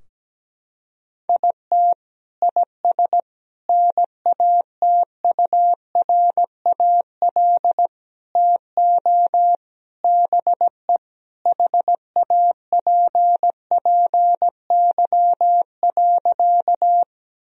Morse code, heard as a keyed tone, about 17 wpm, 700 Hz. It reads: IT IS NATURAL TO BE HAPPY.
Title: Sentences from Top 1000 Words - Review Part 4 - 17wpm